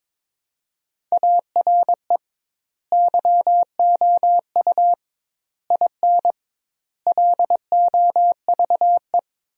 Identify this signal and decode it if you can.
Morse code, heard as a keyed tone, about 22 wpm, 700 Hz. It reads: ARE YOU IN LOVE
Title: 2x Word-Spacing - Sentences from Top 400 Words - 22wpm